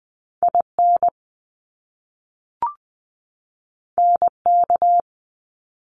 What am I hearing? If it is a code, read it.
Morse code, heard as a keyed tone, about 20 wpm, 700 Hz. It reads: IN NK
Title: Top Two Letter Combinations in Words 20wpm